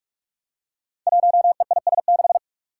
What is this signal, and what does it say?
Morse code, heard as a keyed tone, about 44 wpm, 700 Hz. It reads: 1EIS6
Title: Five Challenging Letter-Number Combinations 45wpm